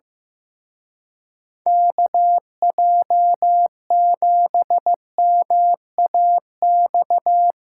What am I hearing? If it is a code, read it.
Morse code, heard as a keyed tone, about 16 wpm, 700 Hz. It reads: KJ7MAX
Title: Speed Racing - USA 2x3 Call Signs 15wpm